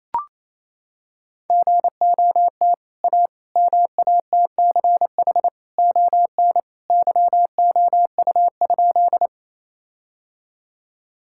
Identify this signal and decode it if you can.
Morse code, heard as a keyed tone, about 27 wpm, 700 Hz. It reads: GOT A MATCH ON YOU?
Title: Sentences from Top 1600 Words 28wpm